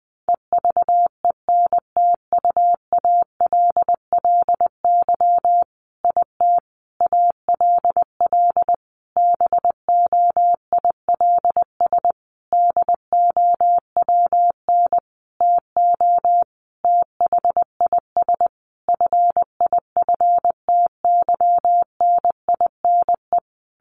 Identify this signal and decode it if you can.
Morse code, heard as a keyed tone, about 20 wpm, 700 Hz. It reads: EVENTUALLY IT ALL BOILS DOWN TO THIS FIFTYNINE
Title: A Sound Of Thunder 20wpm